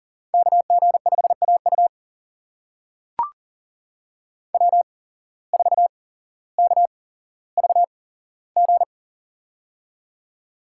Morse code, heard as a keyed tone, about 39 wpm, spaced as for 17 wpm, 700 Hz. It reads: KG5AU W4KVG
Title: USA Call Signs - 5 Characters - 8x Character Spacing - 40wpm